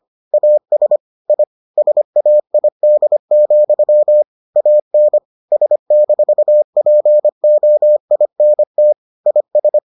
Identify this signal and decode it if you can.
Morse code, heard as a keyed tone, about 25 wpm, 600 Hz. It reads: AS I SAID, AN S-POINT IS